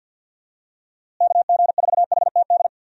Morse code, heard as a keyed tone, about 52 wpm, 700 Hz. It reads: KG4HTD
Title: USA Call Signs - 6 Characters - Repeated 2x - 50wpm